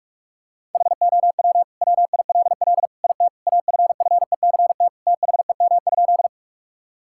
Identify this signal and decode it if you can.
Morse code, heard as a keyed tone, about 46 wpm, 700 Hz. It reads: HOW WILL IT AFFECT THEM?
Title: Sentences from Top 1000 Words - Review Part 6 - 45wpm